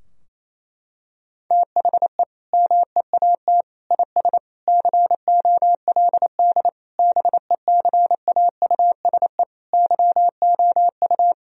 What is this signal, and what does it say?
Morse code, heard as a keyed tone, about 28 wpm, 700 Hz. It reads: THE MEAT IS COLD BECAUSE YOU
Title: Sentences from Top 1000 Words - Review Part 3 - 28wpm